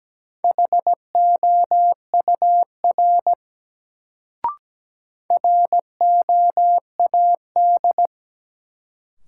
Morse code, heard as a keyed tone, about 17 wpm, 700 Hz. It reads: HOUR ROAD
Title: Sets of 5 Words - New Word Drill - 17wpm